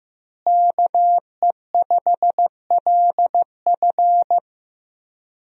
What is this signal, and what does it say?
Morse code, heard as a keyed tone, about 15 wpm, 700 Hz. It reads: KE5LF
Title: USA Call Signs - 5 Characters - Repeated 2x - 15wpm